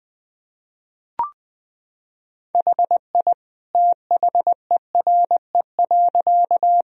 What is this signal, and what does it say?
Morse code, heard as a keyed tone, about 21 wpm, 700 Hz. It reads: HI THERE.
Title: Sentences from Top 1000 Words - Review Part 3 - 20wpm